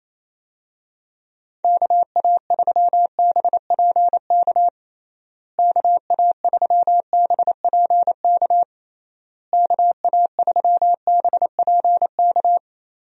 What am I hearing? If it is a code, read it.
Morse code, heard as a keyed tone, about 28 wpm, 700 Hz. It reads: KA3BPK KA3BPK KA3BPK
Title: USA Call Signs - 6 Characters - Repeated 3x - 28wpm